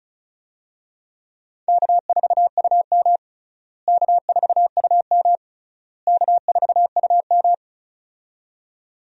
Morse code, heard as a keyed tone, about 36 wpm, 700 Hz. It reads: K4UM K4UM K4UM
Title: USA Call Signs - 4 Characters - Repeated 3x - 35wpm